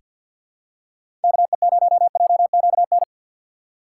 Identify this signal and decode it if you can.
Morse code, heard as a keyed tone, about 49 wpm, 700 Hz. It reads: KE0JQN
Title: USA Call Signs - 6x Character Spacing - 50wpm